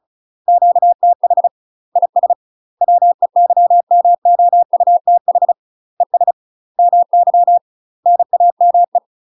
Morse code, heard as a keyed tone, about 34 wpm, 700 Hz. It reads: QTH IS WEYMOUTH ES MY NAME
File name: UK MASTER QSOs TRAINING / 35WPM / UK MASTER QSOs 0043